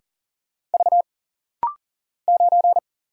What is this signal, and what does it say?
Morse code, heard as a keyed tone, about 40 wpm, 700 Hz. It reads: V 9
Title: Single Letter-Number Alt - Rapid-Fire 40wpm